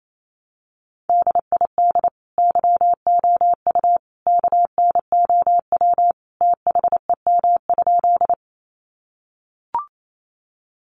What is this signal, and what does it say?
Morse code, heard as a keyed tone, about 28 wpm, 700 Hz. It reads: DID YOU KNOW THEM?
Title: Sentences from Top 100 Words 28wpm